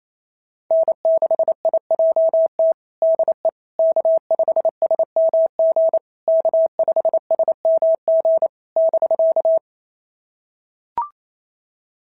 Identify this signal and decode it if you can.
Morse code, heard as a keyed tone, about 25 wpm, 650 Hz. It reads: N6IJT DE K5SMG K5SMG <BK>
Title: SOTA QSO Practice 28wpm